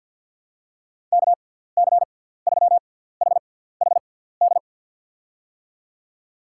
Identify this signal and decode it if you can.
Morse code, heard as a keyed tone, about 53 wpm, spaced as for 17 wpm, 700 Hz. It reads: KC3HHD
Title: USA Call Signs - 6 Characters - 6x Character Spacing - 50wpm